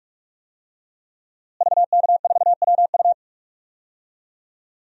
Morse code, heard as a keyed tone, about 46 wpm, 700 Hz. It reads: VK4WU
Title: DXCC-100 Call Signs - 5 Characters - Repeated 3x - 45wpm